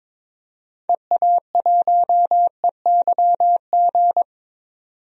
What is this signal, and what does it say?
Morse code, heard as a keyed tone, about 22 wpm, 700 Hz. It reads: EA1EYG
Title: DXCC-100 Call Signs - 6 Characters - 22wpm